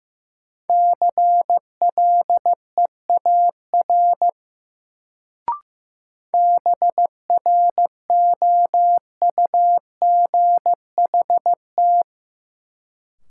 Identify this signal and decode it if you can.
Morse code, heard as a keyed tone, about 15 wpm, 700 Hz. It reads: CLEAR BROUGHT
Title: Sentences from Top 500 Words - New Word Drill - 15wpm